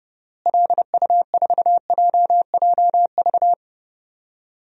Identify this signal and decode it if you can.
Morse code, heard as a keyed tone, about 32 wpm, 700 Hz. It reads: LU4JJV
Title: DXCC-100 Call Signs - 6 Characters - Repeated 3x - 30wpm